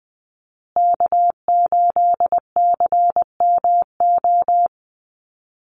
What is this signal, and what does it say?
Morse code, heard as a keyed tone, about 20 wpm, 700 Hz. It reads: K8CMO
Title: Call Signs Repeated 3x - 20wpm